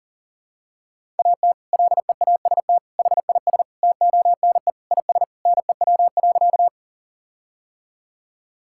Morse code, heard as a keyed tone, about 40 wpm, 700 Hz. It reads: AT LEAST HIS TONE IS NEW.
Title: Sentences from Top 1600 Words 40wpm